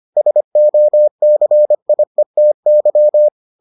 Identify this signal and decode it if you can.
Morse code, heard as a keyed tone, about 25 wpm, 600 Hz. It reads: SOCIETY